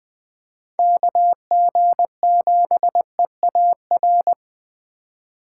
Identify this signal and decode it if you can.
Morse code, heard as a keyed tone, about 21 wpm, 700 Hz. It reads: KG7EAR
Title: Speed Racing - USA 2x3 Call Signs 17wpm